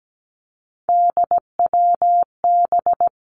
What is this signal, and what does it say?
Morse code, heard as a keyed tone, about 17 wpm, 700 Hz. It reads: DWB
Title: Three Challenging Letter-Number Combinations 17wpm